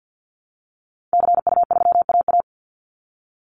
Morse code, heard as a keyed tone, about 52 wpm, 700 Hz. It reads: CU3AA
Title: Call Signs Repeated 3x - 50wpm